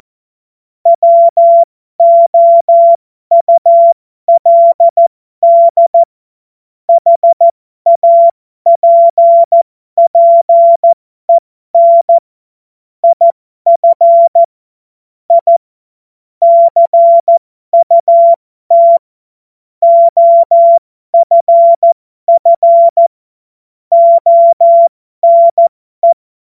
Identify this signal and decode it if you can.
Morse code, heard as a keyed tone, about 14 wpm, spaced as for 12 wpm, 680 Hz. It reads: WOULD HAPPEN IF I CUT OFF ONE